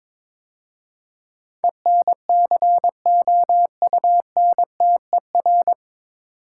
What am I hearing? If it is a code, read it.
Morse code, heard as a keyed tone, about 21 wpm, 700 Hz. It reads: ENCOUNTER
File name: 9 Letter Words - First 3 Letters Given - 22wpm